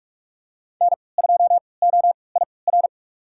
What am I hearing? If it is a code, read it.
Morse code, heard as a keyed tone, about 46 wpm, spaced as for 26 wpm, 700 Hz. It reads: N2OIR
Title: USA Call Signs - 5 Characters - 3x Character Spacing - 45wpm